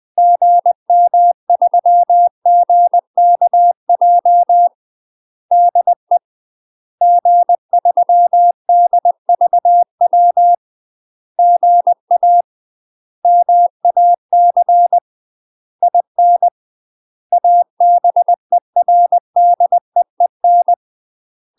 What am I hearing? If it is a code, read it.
Morse code, heard as a keyed tone, about 20 wpm, 700 Hz. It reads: GM3GKJ DE G3DVW GA MAC IN ABERDEEN
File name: UK MASTER QSOs TRAINING / 20WPM / UK MASTER QSOs 0031